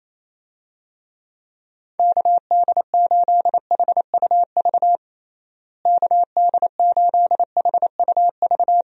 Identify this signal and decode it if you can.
Morse code, heard as a keyed tone, about 28 wpm, 700 Hz. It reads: KD8HUV KD8HUV
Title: USA Call Signs - 6 Characters - Repeated 2x - 28wpm